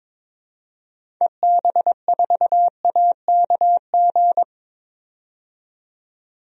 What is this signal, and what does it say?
Morse code, heard as a keyed tone, about 22 wpm, 700 Hz. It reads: EB4AKG
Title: DXCC-100 Call Signs - 6 Characters - 22wpm